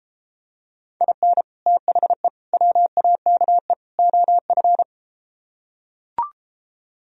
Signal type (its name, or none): Morse code